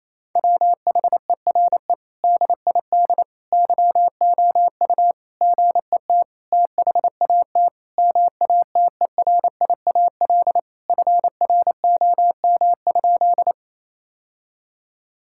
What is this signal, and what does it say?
Morse code, heard as a keyed tone, about 28 wpm, 700 Hz. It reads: WHERE DID YOU GET THAT MATERIAL FROM?
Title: Sentences from Top 1000 Words - Review Part 5 - 28wpm